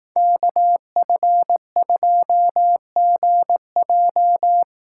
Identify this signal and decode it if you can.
Morse code, heard as a keyed tone, about 19 wpm, 700 Hz. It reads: KF2GJ
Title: Speed Racing - USA Call Signs 15wpm